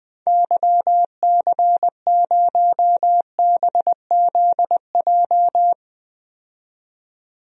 Morse code, heard as a keyed tone, about 21 wpm, 700 Hz. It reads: YC0BZJ
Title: DXCC-100 Call Signs - 6 Characters - 20wpm